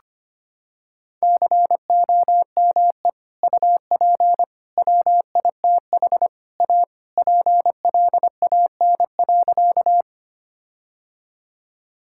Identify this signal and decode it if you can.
Morse code, heard as a keyed tone, about 25 wpm, 700 Hz. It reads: COME UP WITH A PLAN.
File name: practice / 25wpm / Sentences with 4-Word Phrases - 25wpm